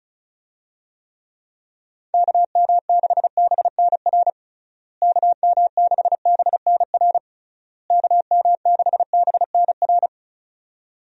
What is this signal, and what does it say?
Morse code, heard as a keyed tone, about 34 wpm, 700 Hz. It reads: KM6BNR KM6BNR KM6BNR
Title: USA Call Signs - 6 Characters - Repeated 3x - 35wpm